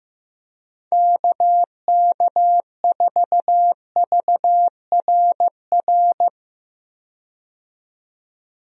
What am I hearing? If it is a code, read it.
Morse code, heard as a keyed tone, about 15 wpm, 700 Hz. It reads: KK4VRR
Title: USA Call Signs - 6 Characters - 15wpm